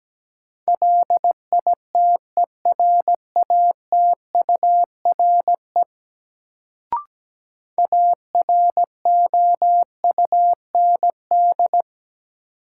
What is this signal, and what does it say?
Morse code, heard as a keyed tone, about 17 wpm, 700 Hz. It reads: LITERATURE AROUND
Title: Sentences from Top 600 Words - New Word Drill - 17wpm